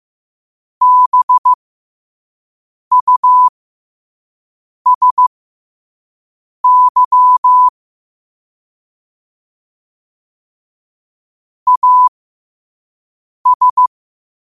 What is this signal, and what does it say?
Morse code, heard as a keyed tone, about 15 wpm, spaced as for 5 wpm, 1000 Hz. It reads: BUSY AS
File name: qotdfiles / 5wpm